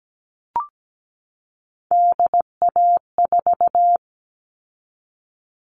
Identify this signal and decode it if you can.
Morse code, heard as a keyed tone, about 17 wpm, 700 Hz. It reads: DA4
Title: All Cut Numbers - 3-Digit 17wpm